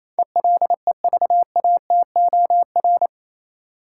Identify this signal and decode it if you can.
Morse code, heard as a keyed tone, about 27 wpm, 700 Hz. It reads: ELEVATOR
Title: Sentences from Top 1600 Words - New Word Drill - 28wpm